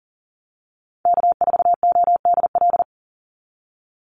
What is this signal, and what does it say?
Morse code, heard as a keyed tone, about 40 wpm, 700 Hz. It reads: K4ODL
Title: Call Signs Repeated 2x - 40wpm